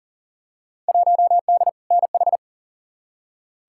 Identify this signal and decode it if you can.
Morse code, heard as a keyed tone, about 40 wpm, 700 Hz. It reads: 1D NH
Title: Field Day Exchanges Encore - 40wpm